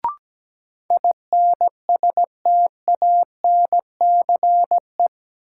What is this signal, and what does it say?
Morse code, heard as a keyed tone, about 17 wpm, 700 Hz. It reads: INSTANCE
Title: Sentences from Top 800 Words - New Word Drill - 17wpm